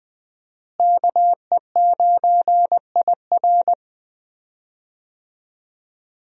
Morse code, heard as a keyed tone, about 20 wpm, 700 Hz. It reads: KE9IR
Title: USA Call Signs - 5 Characters - Repeated 2x - 20wpm